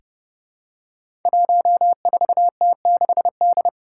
Morse code, heard as a keyed tone, about 31 wpm, 700 Hz. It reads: 14T6D
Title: Five Challenging Letter-Number Combinations 30wpm